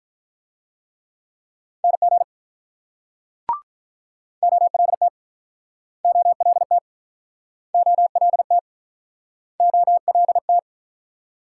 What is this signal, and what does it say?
Morse code, heard as a keyed tone, about 48 wpm, 700 Hz. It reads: NG OLT OLT OLT OLT